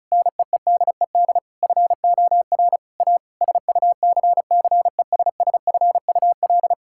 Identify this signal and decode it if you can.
Morse code, heard as a keyed tone, about 36 wpm, 700 Hz. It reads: NEEDED FOR A SUCCESSFUL
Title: Sentences from Top 1000 Words - Review Part 6 - 35wpm